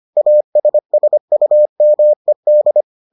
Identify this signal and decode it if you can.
Morse code, heard as a keyed tone, about 25 wpm, 600 Hz. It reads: ASSUMED